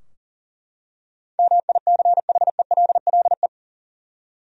Morse code, heard as a keyed tone, about 40 wpm, 700 Hz. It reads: MICHELLE
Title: USA Common Names - 40wpm